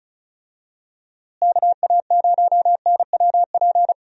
Morse code, heard as a keyed tone, about 36 wpm, 700 Hz. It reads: KA0NWP
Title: USA Call Signs - 6 Characters - Repeated 2x - 35wpm